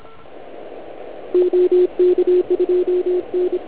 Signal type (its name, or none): Morse code